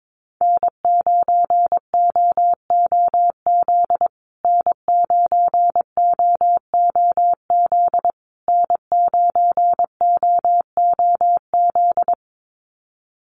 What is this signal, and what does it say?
Morse code, heard as a keyed tone, about 22 wpm, 700 Hz. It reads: N9OOZ N9OOZ N9OOZ
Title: USA Call Signs Repeated 3x 22wpm